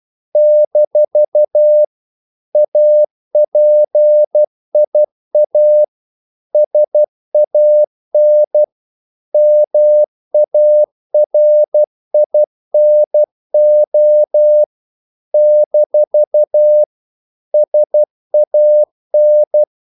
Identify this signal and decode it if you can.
Morse code, heard as a keyed tone, about 12 wpm, 600 Hz. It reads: - APIA SAN MARINO - SAN
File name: CW / Practice / NationalCapitals12